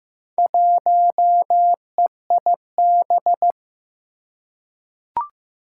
Morse code, heard as a keyed tone, about 15 wpm, 700 Hz. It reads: 1EIB
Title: Four Challenging Letter-Number Combinations 15wpm